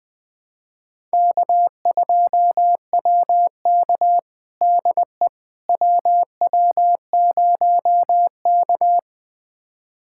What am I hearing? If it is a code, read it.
Morse code, heard as a keyed tone, about 20 wpm, 700 Hz. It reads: K2WK DE WW0K
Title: USA Callsign DE Callsign - 4 Characters - 20wpm